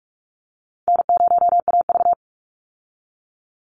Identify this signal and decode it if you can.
Morse code, heard as a keyed tone, about 44 wpm, 700 Hz. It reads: N0AV
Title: Easy Call Signs Repeated 3x - 45wpm